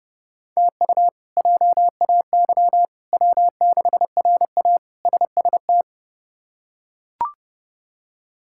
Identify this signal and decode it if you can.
Morse code, heard as a keyed tone, about 29 wpm, 700 Hz. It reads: TU JAY W6RA SST
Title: K1USN SST QSO Practice - 30wpm